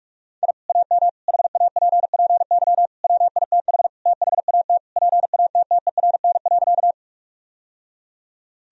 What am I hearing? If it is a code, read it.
Morse code, heard as a keyed tone, about 46 wpm, 700 Hz. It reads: I AM HAPPY WITH THAT PATTERN.
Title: Sentences from Top 1000 Words - Review Part 2 - 45wpm